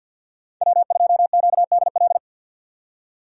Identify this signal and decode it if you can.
Morse code, heard as a keyed tone, about 53 wpm, 700 Hz. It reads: W2QDL
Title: USA Call Signs - 4x Character Spacing - 50wpm